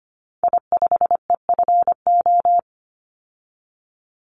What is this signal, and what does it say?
Morse code, heard as a keyed tone, about 25 wpm, 700 Hz. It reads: I5EFO
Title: Call Signs Repeated 2x - 25wpm